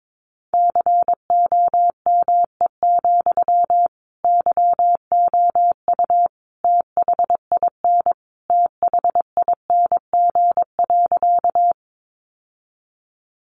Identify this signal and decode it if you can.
Morse code, heard as a keyed tone, about 22 wpm, 700 Hz. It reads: COME, YOU THIN THING.
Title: Sentences from Top 1200 Words 22wpm